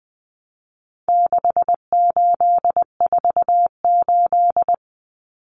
Morse code, heard as a keyed tone, about 20 wpm, 700 Hz. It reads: 6848
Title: Numbers - 4-Digit 20wpm